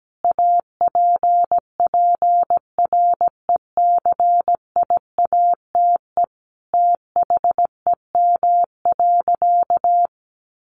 Morse code, heard as a keyed tone, about 17 wpm, 700 Hz. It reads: APPRECIATE THEM.
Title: Sentences from Top 1300 Words 17wpm